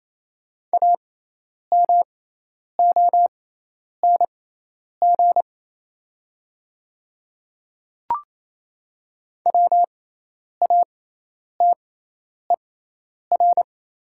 Morse code, heard as a keyed tone, about 28 wpm, spaced as for 10 wpm, 700 Hz. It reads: AMONG WATER
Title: 6x Character Spacing - Top 500 Words - 28wpm